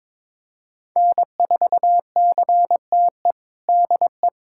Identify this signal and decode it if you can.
Morse code, heard as a keyed tone, about 22 wpm, 700 Hz. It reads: N4CTE DE
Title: USA Callsign DE Callsign 22wpm